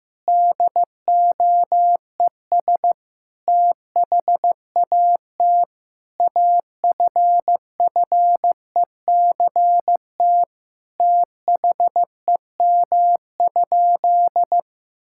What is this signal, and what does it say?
Morse code, heard as a keyed tone, about 15 wpm, 700 Hz. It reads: DOES THAT AFFECT THEM?
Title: Sentences from Top 1000 Words - Review Part 6 - 15wpm